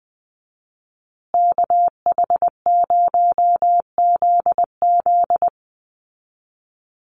Morse code, heard as a keyed tone, about 20 wpm, 700 Hz. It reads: KH0ZZ
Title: Call Signs Repeated 2x - 20wpm